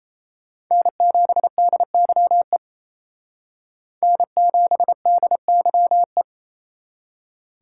Morse code, heard as a keyed tone, about 36 wpm, 700 Hz. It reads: N7DYE N7DYE